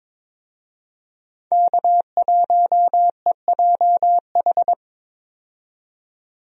Morse code, heard as a keyed tone, about 22 wpm, 700 Hz. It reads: K1EJH
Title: USA Call Signs - 5 Characters - 22wpm